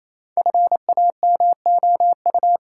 Morse code, heard as a keyed tone, about 28 wpm, 700 Hz. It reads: FAMOU
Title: Sentences from Top 1500 Words - New Word Drill - 28wpm